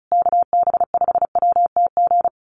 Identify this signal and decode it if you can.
Morse code, heard as a keyed tone, about 34 wpm, 700 Hz. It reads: KB5WTG
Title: USA Call Signs Repeated 3x 35wpm